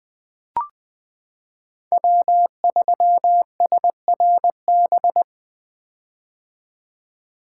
Morse code, heard as a keyed tone, about 20 wpm, 700 Hz. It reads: W3SRB
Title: 2023 POTA Activator Call Signs - 20wpm